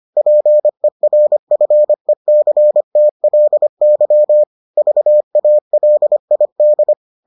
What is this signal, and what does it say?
Morse code, heard as a keyed tone, about 25 wpm, 600 Hz. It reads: PERFECTLY VALID